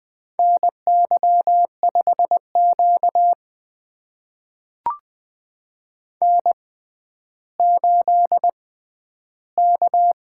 Morse code, heard as a keyed tone, about 20 wpm, spaced as for 11 wpm, 700 Hz. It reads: NY5Q N8K
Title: USA Call Signs - 4 Characters - 6x Character Spacing - 20wpm